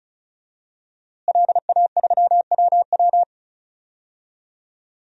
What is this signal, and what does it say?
Morse code, heard as a keyed tone, about 34 wpm, 700 Hz. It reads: LA3WW
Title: DXCC-100 Call Signs - 5 Characters - Repeated 2x - 35wpm